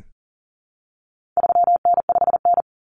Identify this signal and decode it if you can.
Morse code, heard as a keyed tone, about 40 wpm, 700 Hz. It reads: 3N5N
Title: Common Cut Numbers - 4-Digit 40wpm